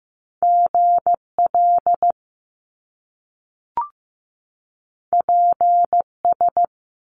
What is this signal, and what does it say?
Morse code, heard as a keyed tone, about 15 wpm, 700 Hz. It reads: GL PS
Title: Top Two Letter Combinations in Words 15wpm